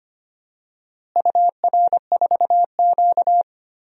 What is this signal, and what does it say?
Morse code, heard as a keyed tone, about 25 wpm, 700 Hz. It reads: UR4Q
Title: DXCC-100 Call Signs - 4 Characters - Repeated 2x - 25wpm